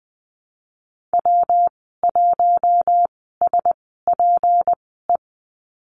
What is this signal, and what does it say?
Morse code, heard as a keyed tone, about 19 wpm, spaced as for 14 wpm, 700 Hz. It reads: W1SPE
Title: 154 Lesson 25g - Letter p - Call Signs 20wpm